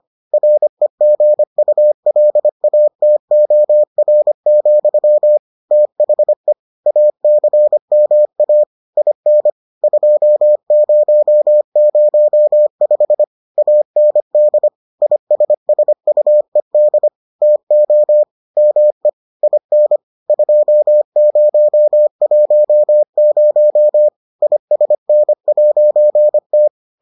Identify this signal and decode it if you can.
Morse code, heard as a keyed tone, about 25 wpm, 600 Hz. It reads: REGULATOR, THE ACMA IN 2005 AND ISSUED TO ME IN 2010 ISN'T